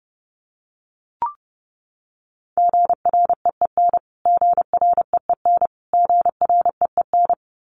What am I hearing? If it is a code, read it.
Morse code, heard as a keyed tone, about 30 wpm, 700 Hz. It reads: GREEN GREEN GREEN
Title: Top 500 Words Repeated 30wpm